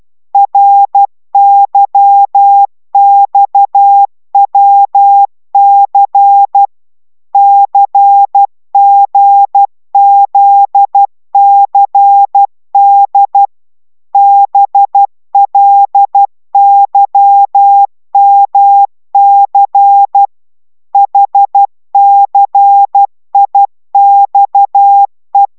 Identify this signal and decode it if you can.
Morse code, heard as a keyed tone, about 12 wpm, 800 Hz. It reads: RYXWC CGZCD BLYMC HCIXE
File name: taviro tanulas / cw 69 19 10